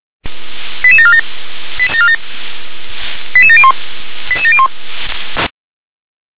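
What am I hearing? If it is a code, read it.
Five-tone selective call (ZVEI), heard as quick runs of five tones, tones of about 80 ms. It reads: 88747 88701 88701
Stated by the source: ZVEI five-tone selective call 88852 received via SDR.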